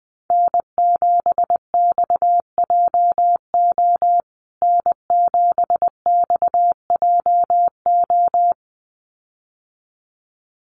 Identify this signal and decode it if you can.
Morse code, heard as a keyed tone, about 20 wpm, 700 Hz. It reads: N7XJO N7XJO
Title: USA Call Signs Repeated 2x 20wpm